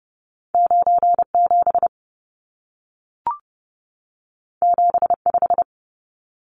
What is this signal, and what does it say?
Morse code, heard as a keyed tone, about 30 wpm, 700 Hz. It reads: 97 75
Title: Numbers - 2-Digit 30wpm